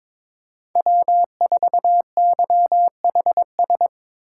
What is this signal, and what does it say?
Morse code, heard as a keyed tone, about 22 wpm, 700 Hz. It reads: W4YHS
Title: USA Call Signs - 2x Character Spacing - 22wpm